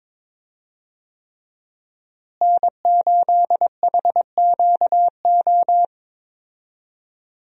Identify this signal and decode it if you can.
Morse code, heard as a keyed tone, about 22 wpm, 700 Hz. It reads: N8HQO
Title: USA Call Signs - 5 Characters - Repeated 3x - 22wpm